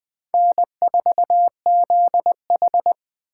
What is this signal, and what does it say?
Morse code, heard as a keyed tone, about 20 wpm, 700 Hz. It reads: N4ZH